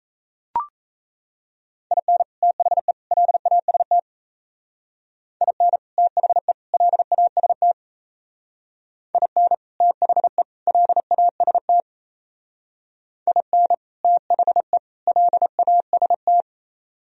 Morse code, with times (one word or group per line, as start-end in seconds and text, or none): 1.9-2.2 IN
2.4-2.9 THE
3.1-4.0 LAST
5.4-5.8 IN
6.0-6.5 THE
6.7-7.7 LAST
9.2-9.5 IN
9.8-10.4 THE
10.7-11.8 LAST
13.3-13.7 IN
14.0-14.8 THE
15.1-16.4 LAST